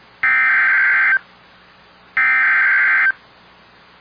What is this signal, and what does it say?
Emergency Alert System (SAME) tones (header bursts), heard as harsh buzzing bursts. It is heard from 0.2 to 3.1 s.